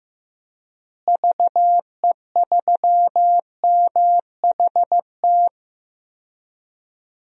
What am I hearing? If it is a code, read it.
Morse code, heard as a keyed tone, about 15 wpm, 700 Hz. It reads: VE3MHT
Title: DXCC-100 Call Signs - 6 Characters - Repeated 3x - 15wpm